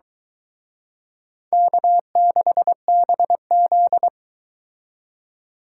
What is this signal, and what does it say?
Morse code, heard as a keyed tone, about 23 wpm, 700 Hz. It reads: K6BZ